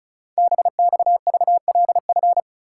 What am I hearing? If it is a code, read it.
Morse code, heard as a keyed tone, about 36 wpm, 700 Hz. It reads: BXVLF
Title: Code Groups - 5 Letters - 35wpm